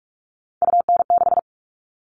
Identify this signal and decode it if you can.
Morse code, heard as a keyed tone, about 44 wpm, 700 Hz. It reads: UN6
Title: All Cut Numbers - 3-Digit 45wpm